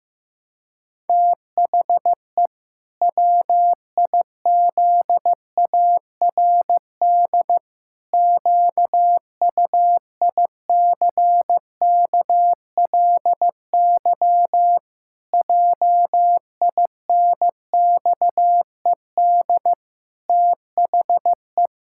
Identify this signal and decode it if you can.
Morse code, heard as a keyed tone, about 15 wpm, 700 Hz. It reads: THE WIZARD QUICKLY JINXED THE
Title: Pangram Challenge v2 - 15wpm